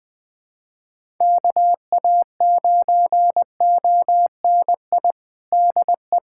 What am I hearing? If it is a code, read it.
Morse code, heard as a keyed tone, about 20 wpm, 700 Hz. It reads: KA9ONI DE
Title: USA Callsign DE Callsign - 6 Characters - 20wpm